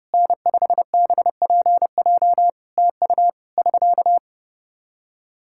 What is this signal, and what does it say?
Morse code, heard as a keyed tone, about 30 wpm, 700 Hz. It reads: N5BPJ TU <SK>
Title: K1USN SST QSO Practice 30wpm